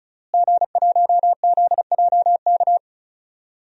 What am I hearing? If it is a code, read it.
Morse code, heard as a keyed tone, about 34 wpm, 700 Hz. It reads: G1ZJK
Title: DXCC-100 Call Signs - 5 Characters - Repeated 3x - 35wpm